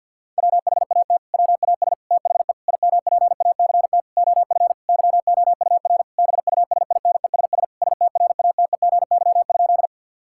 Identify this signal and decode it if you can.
Morse code, heard as a keyed tone, about 50 wpm, 700 Hz. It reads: WHAT WAS THE IMPACT OF YOUR BUSINESS STRATEGY?
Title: Sentences from Top 1000 Words - Review Part 6 - 50wpm